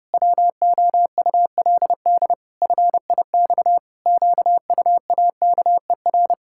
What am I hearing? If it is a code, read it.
Morse code, heard as a keyed tone, about 30 wpm, 700 Hz. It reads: WOULD FIX QUAKER
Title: Pangram Challenge v2 - 30wpm